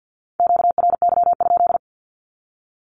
Morse code, heard as a keyed tone, about 49 wpm, 700 Hz. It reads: QRY?
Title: Q-Codes 50wpm